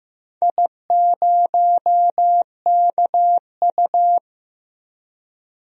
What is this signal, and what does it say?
Morse code, heard as a keyed tone, about 15 wpm, 700 Hz. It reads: I0KU
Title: DXCC-100 Call Signs - 4 Characters - 15wpm